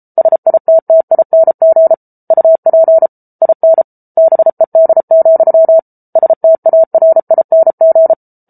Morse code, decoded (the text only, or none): SITTING UP IN BED, STARING